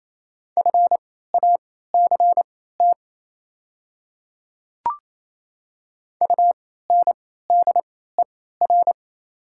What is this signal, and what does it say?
Morse code, heard as a keyed tone, about 28 wpm, spaced as for 16 wpm, 700 Hz. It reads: FACT UNDER
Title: 3x Character Spacing - Top 500 Words - 28wpm